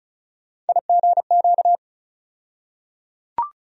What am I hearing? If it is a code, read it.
Morse code, heard as a keyed tone, about 36 wpm, 700 Hz. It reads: IGQ
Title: USA Call Sign Postfixes - 3 Letters 35wpm